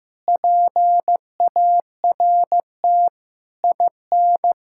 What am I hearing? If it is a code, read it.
Morse code, heard as a keyed tone, about 16 wpm, 700 Hz. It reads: PART IN
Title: Sentences from Top 1000 Words - Review Part 6 - 15wpm